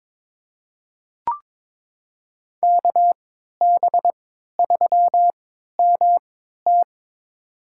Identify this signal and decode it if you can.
Morse code, heard as a keyed tone, about 21 wpm, spaced as for 12 wpm, 700 Hz. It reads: KB3MT
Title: USA Call Signs - 3x Character Spacing - 22wpm